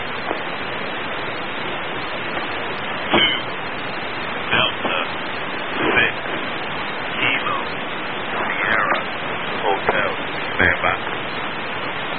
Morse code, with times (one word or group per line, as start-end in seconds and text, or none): none